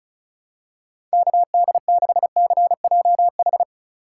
Morse code, heard as a keyed tone, about 34 wpm, 700 Hz. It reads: KD6CJH
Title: USA Call Signs - 6 Characters - 2x Character Spacing - 35wpm